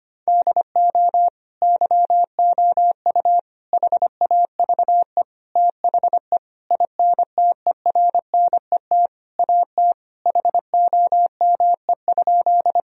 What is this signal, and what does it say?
Morse code, heard as a keyed tone, about 25 wpm, 700 Hz. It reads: DO YOU HAVE THE INTERNET AT HOME?
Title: Sentences from Top 1000 Words - Review Part 4 - 25wpm